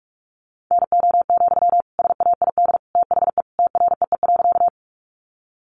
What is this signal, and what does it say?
Morse code, heard as a keyed tone, about 44 wpm, 700 Hz. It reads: NO, SAID THE TREE.
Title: Sentences from Top 300 Words 45wpm